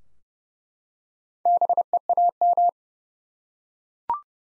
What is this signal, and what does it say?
Morse code, heard as a keyed tone, about 30 wpm, 700 Hz. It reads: BEAM
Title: Top 90 Abbr 30WPM